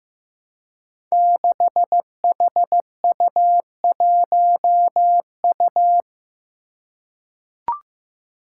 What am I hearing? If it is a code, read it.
Morse code, heard as a keyed tone, about 15 wpm, 700 Hz. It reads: 6HU1U
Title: Five Challenging Letter-Number Combinations 15wpm